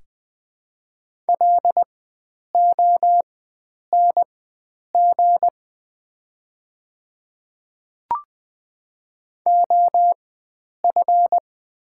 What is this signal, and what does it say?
Morse code, heard as a keyed tone, about 20 wpm, spaced as for 9 wpm, 700 Hz. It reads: LONG OF
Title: 4x Character Spacing - Top 100 Words - 20wpm